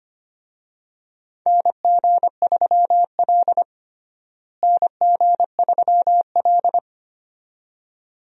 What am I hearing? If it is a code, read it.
Morse code, heard as a keyed tone, about 25 wpm, 700 Hz. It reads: NG3L NG3L
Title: USA Call Signs - 4 Characters - Repeated 2x - 25wpm